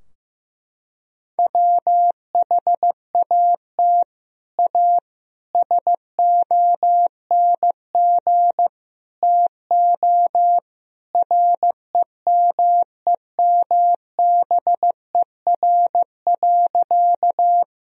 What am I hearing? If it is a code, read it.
Morse code, heard as a keyed tone, about 16 wpm, 700 Hz. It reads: WHAT A SONG TO REMEMBER.
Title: Sentences from Top 1000 Words - Review Part 2 - 15wpm